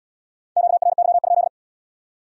Morse code, heard as a keyed tone, about 74 wpm, 700 Hz. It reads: QRY?